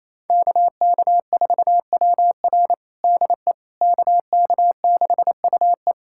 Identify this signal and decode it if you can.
Morse code, heard as a keyed tone, about 28 wpm, 700 Hz. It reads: KK4WR DE KK6UE
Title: USA Callsign DE Callsign - 5 Characters - 28wpm